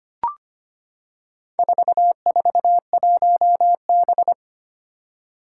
Morse code, heard as a keyed tone, about 25 wpm, 700 Hz. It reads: 441B
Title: Four Challenging Letter-Number Combinations 25wpm